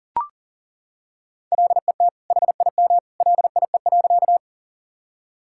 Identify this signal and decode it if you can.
Morse code, heard as a keyed tone, about 40 wpm, 700 Hz. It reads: LET HIM LIE.
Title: Sentences from Top 1600 Words 40wpm